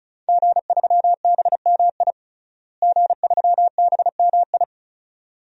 Morse code, heard as a keyed tone, about 36 wpm, 700 Hz. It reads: G3BMI G3BMI
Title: DXCC-100 Call Signs - 5 Characters - Repeated 2x - 35wpm